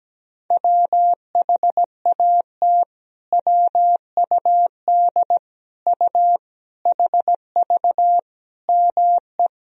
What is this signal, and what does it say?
Morse code, heard as a keyed tone, about 17 wpm, 700 Hz. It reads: WHAT WUD U HV ME
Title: Sentences from Top 200 Words and Abbreviations 17wpm